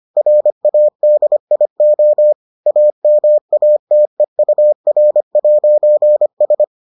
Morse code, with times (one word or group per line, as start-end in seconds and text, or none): 0.2-2.3 RADIO
2.7-6.6 AMATEUR'S